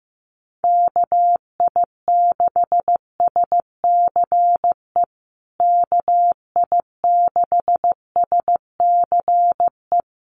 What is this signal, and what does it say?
Morse code, heard as a keyed tone, about 14 wpm, 700 Hz. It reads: KI6SCE KI6SCE
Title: USA Call Signs Repeated 2x 15wpm